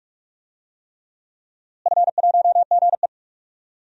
Morse code, heard as a keyed tone, about 44 wpm, 700 Hz. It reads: F1GE
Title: DXCC-100 Call Signs - 4 Characters - Repeated 3x - 45wpm